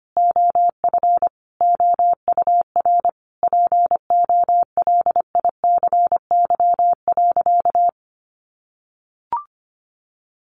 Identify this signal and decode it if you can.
Morse code, heard as a keyed tone, about 25 wpm, 700 Hz. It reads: OF OUR POLICY.
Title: Sentences from Top 1200 Words 25wpm